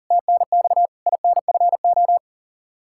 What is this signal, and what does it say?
Morse code, heard as a keyed tone, about 40 wpm, 690 Hz. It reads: TNX INFO
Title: Practice QSO - Exchange 3 - More Info - 40wpm